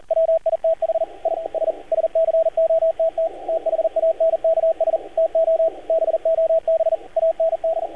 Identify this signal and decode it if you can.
Morse code, heard as a keyed tone, about 38 wpm, 650 Hz, with radio noise. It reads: WITH I S SCOTT THANKS TO BOB AND
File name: archived / CRSnet